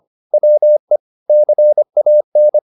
Morse code, heard as a keyed tone, about 25 wpm, 600 Hz. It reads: WE CAN
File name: the old man and the sea part 1 25wpm